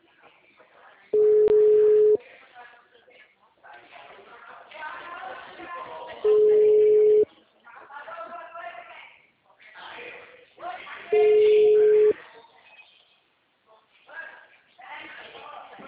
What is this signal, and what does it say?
Ringback tone (Europe, 425 Hz), with speech.
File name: out- -2010- - - .4429